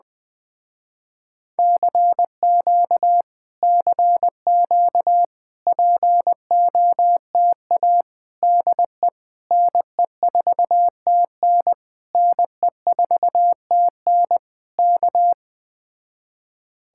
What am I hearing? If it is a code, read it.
Morse code, heard as a keyed tone, about 20 wpm, 700 Hz. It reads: CQ CQ POTA DE NE4TN NE4TN K